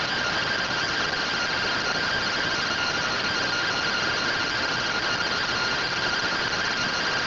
Two-tone teleprinter signal, about 170 Hz shift.